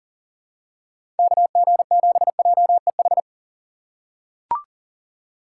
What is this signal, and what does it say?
Morse code, heard as a keyed tone, about 40 wpm, 700 Hz. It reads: KG7JEH
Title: USA Call Signs - 3x Character Spacing - 40wpm